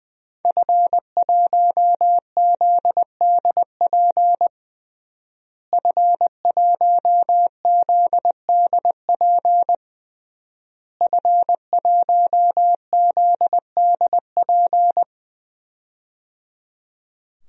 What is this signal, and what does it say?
Morse code, heard as a keyed tone, about 20 wpm, 700 Hz. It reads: F1ZDP F1ZDP F1ZDP
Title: DXCC-100 Call Signs - 5 Characters - Repeated 3x - 20wpm